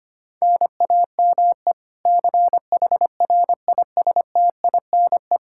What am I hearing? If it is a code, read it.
Morse code, heard as a keyed tone, about 25 wpm, 700 Hz. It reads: NAME CHRISTINE
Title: Practice QSO - Exchange 1 - Beginning - 25wpm